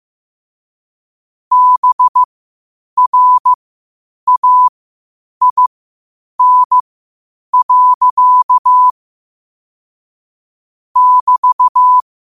Morse code, heard as a keyed tone, about 15 wpm, spaced as for 8 wpm, 1000 Hz. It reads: BRAIN. <BT>
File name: qotdfiles / 8wpm